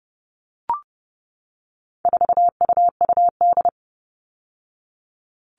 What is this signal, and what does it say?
Morse code, heard as a keyed tone, about 30 wpm, 700 Hz. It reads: 4UUD
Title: All Cut Numbers - 4-Digit 30wpm